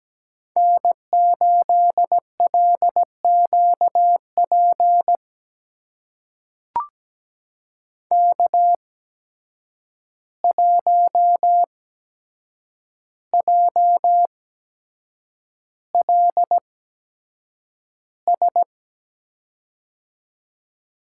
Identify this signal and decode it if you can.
Morse code, heard as a keyed tone, about 17 wpm, spaced as for 7 wpm, 700 Hz. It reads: N8LQP K1JLS
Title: USA Call Signs - 5 Characters - 8x Character Spacing - 17wpm